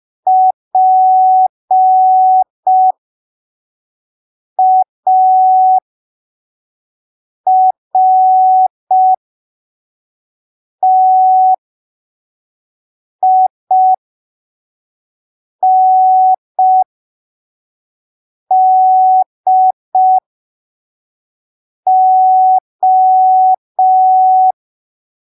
Morse code, heard as a keyed tone, about 5 wpm, spaced as for 3 wpm, 750 Hz. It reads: PARTINDO
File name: arquivos / texto / texto 08 05ppm